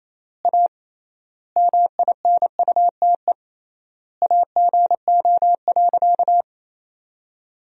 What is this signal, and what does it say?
Morse code, heard as a keyed tone, about 28 wpm, 700 Hz. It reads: A MINUTE AGO.
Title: 3x Word-Spacing - Sentences from Top 500 Words - 28wpm